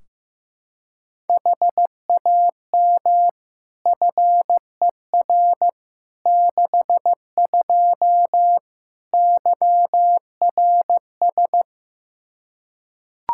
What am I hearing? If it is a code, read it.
Morse code, heard as a keyed tone, about 15 wpm, 700 Hz. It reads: HAM FER 62 YRS
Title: K1USN SST QSO Practice 15wpm